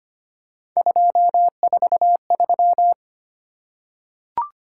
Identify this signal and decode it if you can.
Morse code, heard as a keyed tone, about 25 wpm, 700 Hz. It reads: 243
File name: Three Letter-Number Combinations - 25wpm